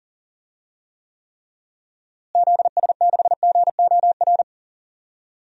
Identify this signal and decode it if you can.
Morse code, heard as a keyed tone, about 40 wpm, 700 Hz. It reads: ZS6GOR
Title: DXCC-100 Call Signs - 6 Characters - Repeated 3x - 40wpm